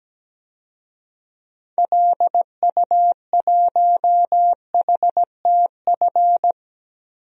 Morse code, heard as a keyed tone, about 17 wpm, 700 Hz. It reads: LU1HTF
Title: DXCC-100 Call Signs - 6 Characters - Repeated 2x - 17wpm